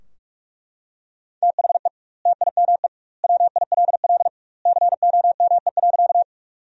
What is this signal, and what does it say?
Morse code, heard as a keyed tone, about 41 wpm, 700 Hz. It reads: THE TIME WILL COME.
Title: 2x Word-Spacing - Sentences from Top 100 Words - 45wpm